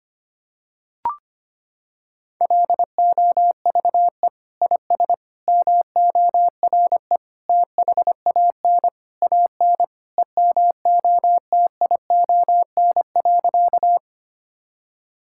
Morse code, heard as a keyed tone, about 25 wpm, 700 Hz. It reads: LOVE IS MORE THAN AN EMOTION.
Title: Sentences from Top 1600 Words 25wpm